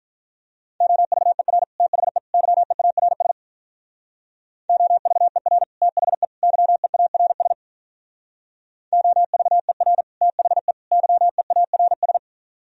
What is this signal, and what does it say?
Morse code, heard as a keyed tone, about 44 wpm, 700 Hz. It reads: OVER THE YEARS OVER THE YEARS OVER THE YEARS
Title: Speed-Racing - Sets of 3 Words 35wpm